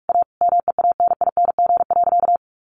Morse code, heard as a keyed tone, about 44 wpm, 700 Hz. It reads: A MEANING.
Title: Sentences from Top 1400 Words 45wpm